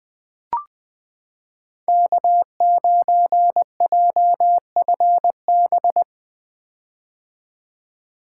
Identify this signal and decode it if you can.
Morse code, heard as a keyed tone, about 21 wpm, 700 Hz. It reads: K9JFB
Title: 2023 POTA Activator Call Signs - 20wpm